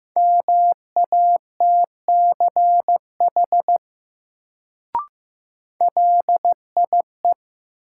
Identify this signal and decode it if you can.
Morse code, heard as a keyed tone, about 15 wpm, 700 Hz. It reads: MATCH LIE
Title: Sentences from Top 1600 Words - New Word Drill - 15wpm